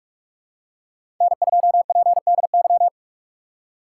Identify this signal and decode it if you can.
Morse code, heard as a keyed tone, about 42 wpm, 690 Hz. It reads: N2PDY
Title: USA Call Signs - 5 Characters - 4x Character Spacing - 45wpm